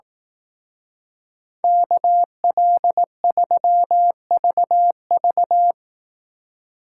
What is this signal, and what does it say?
Morse code, heard as a keyed tone, about 18 wpm, 700 Hz. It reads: KL3VV
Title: Speed Racing - USA 2x2 Call Signs 15wpm